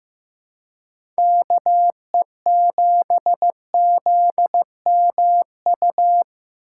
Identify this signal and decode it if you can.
Morse code, heard as a keyed tone, about 15 wpm, 700 Hz. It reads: KE7ZMU
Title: USA Call Signs - 6 Characters - 3x Character Spacing - 15wpm